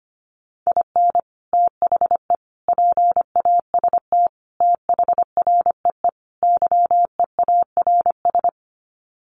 Morse code, heard as a keyed tone, about 25 wpm, 700 Hz. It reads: IN THE PAST THREE YEARS
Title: Sets of 5 Words Encore2 25wpm